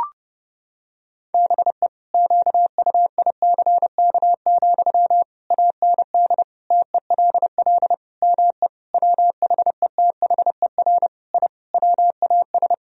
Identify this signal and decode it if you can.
Morse code, heard as a keyed tone, about 30 wpm, 700 Hz. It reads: BE QUICK, AND TELL ME WHETHER I WAS
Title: Sentences from Top 1000 Words - Review Part 3 - 30wpm